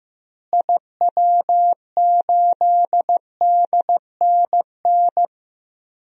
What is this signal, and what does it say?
Morse code, heard as a keyed tone, about 15 wpm, 700 Hz. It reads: IW8DNN
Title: DXCC-100 Call Signs - 6 Characters - Repeated 3x - 15wpm